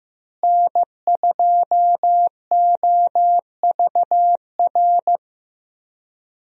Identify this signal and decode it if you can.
Morse code, heard as a keyed tone, about 15 wpm, 700 Hz. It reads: N2OVR
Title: USA Call Signs - 5 Characters - Repeated 3x - 15wpm